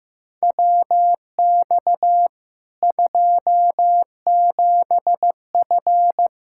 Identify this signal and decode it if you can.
Morse code, heard as a keyed tone, about 15 wpm, 700 Hz. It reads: WX 27F
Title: K1USN SST QSO Practice 15wpm